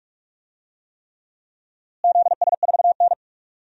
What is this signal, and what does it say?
Morse code, heard as a keyed tone, about 42 wpm, 700 Hz. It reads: ZS4N
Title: DXCC-100 Call Signs - 4 Characters - Repeated 3x - 45wpm